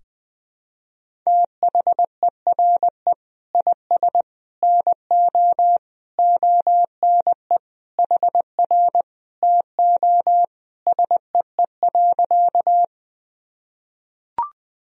Morse code, heard as a keyed tone, about 20 wpm, 700 Hz. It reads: THERE IS NO ONE HR TO SEE.
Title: Sentences from Top 200 Words and Abbreviations 20wpm